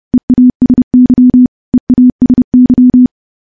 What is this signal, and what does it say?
Morse code, heard as a keyed tone, about 30 wpm, 250 Hz. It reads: EASY EASY
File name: morse / morse-training-30WPM